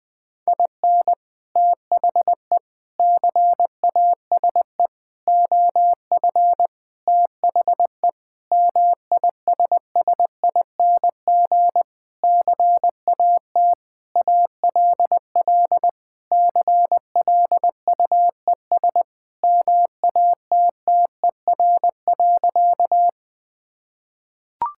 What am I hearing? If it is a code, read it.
Morse code, heard as a keyed tone, about 21 wpm, 690 Hz. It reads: IN THE CASE OF THE MISSING CAT ALL CLUES MATTER.
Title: Sentences with 4-Word Phrases - Spoken Beginning - 20wpm